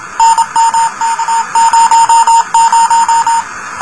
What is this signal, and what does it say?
Morse code, heard as a keyed tone, about 27 wpm, 910 Hz, with noise. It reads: NMK00
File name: luxorion / Radio / rs12-23mar02-telemetry-cw